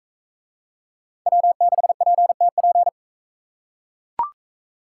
Morse code, heard as a keyed tone, about 44 wpm, 700 Hz. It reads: W6PTP